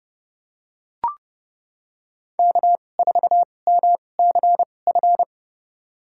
Morse code, heard as a keyed tone, about 29 wpm, spaced as for 22 wpm, 700 Hz. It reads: K4MCF
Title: USA Call Signs - 5 Characters - 2x Character Spacing - 30wpm